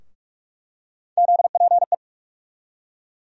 Morse code, heard as a keyed tone, about 46 wpm, 700 Hz. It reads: ZPE
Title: USA Call Sign Postfixes - 3 Letters 45wpm